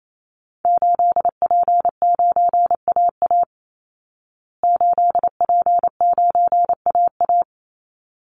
Morse code, heard as a keyed tone, about 28 wpm, 700 Hz. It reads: 8P9AA 8P9AA
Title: Call Signs Repeated 3x - 28wpm